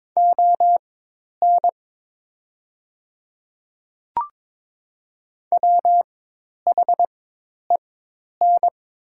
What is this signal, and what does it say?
Morse code, heard as a keyed tone, about 23 wpm, spaced as for 10 wpm, 700 Hz. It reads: ON WHEN
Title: 4x Character Spacing - Top 100 Words - 22wpm